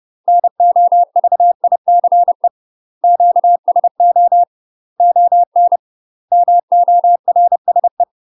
Morse code, heard as a keyed tone, about 30 wpm, 700 Hz. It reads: NOVICE QSO ON MORSE
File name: UK MASTER QSOs TRAINING / 30WPM / UK MASTER QSOs 0005